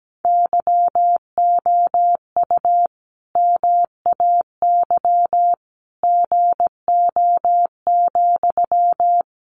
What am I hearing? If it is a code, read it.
Morse code, heard as a keyed tone, about 17 wpm, 700 Hz. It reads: YOU MAY GO,
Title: Sentences from Top 400 Words 17wpm